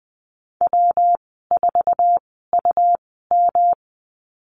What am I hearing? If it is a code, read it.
Morse code, heard as a keyed tone, about 20 wpm, spaced as for 14 wpm, 700 Hz. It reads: W4UM
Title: 128 Lesson 21g - Number 6 - Call Signs 20wpm